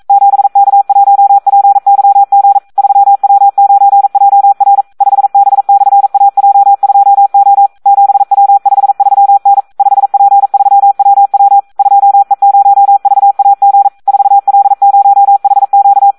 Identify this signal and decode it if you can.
Morse code, heard as a keyed tone, about 42 wpm, 800 Hz. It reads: 7K1PYG 3W9JR 5B/AJ2O 7W53N 5P3WW 2E0VAG 4L0HQ
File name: Audio practica CW / SD 1 42